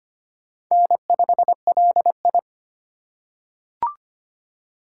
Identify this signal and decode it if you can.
Morse code, heard as a keyed tone, about 25 wpm, 700 Hz. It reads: N5LI
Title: USA Call Signs - 3x Character Spacing - 25wpm